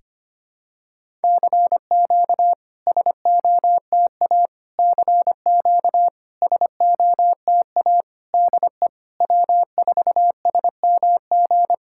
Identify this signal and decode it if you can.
Morse code, heard as a keyed tone, about 25 wpm, 700 Hz. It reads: CQ SOTA CQ SOTA DE W4SMG
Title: SOTA QSO Practice 25wpm